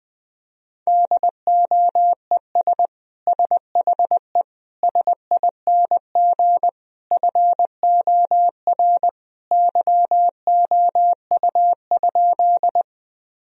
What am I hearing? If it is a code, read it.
Morse code, heard as a keyed tone, about 20 wpm, 700 Hz. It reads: DOES SHE SING FOR YOU?
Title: Sentences from Top 1000 Words - Review Part 2 - 20wpm